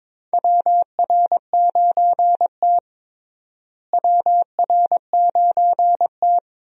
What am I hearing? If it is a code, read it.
Morse code, heard as a keyed tone, about 21 wpm, 700 Hz. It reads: WR9T WR9T
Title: USA Call Signs - 4 Characters - Repeated 2x - 22wpm